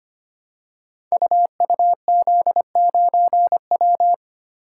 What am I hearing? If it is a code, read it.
Morse code, heard as a keyed tone, about 25 wpm, 700 Hz. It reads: UUZ9W
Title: Code Groups - 5 Letter:Numbers - 25wpm